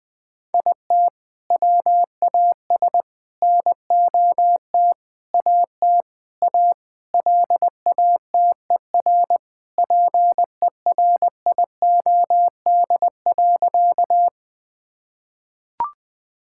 Morse code, heard as a keyed tone, about 20 wpm, 700 Hz. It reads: IT WAS NOT AT A LATER PERIOD.
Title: Sentences from Top 1000 Words - Review Part 5 - 20wpm